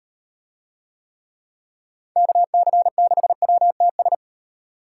Morse code, heard as a keyed tone, about 38 wpm, 700 Hz. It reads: KC6WTS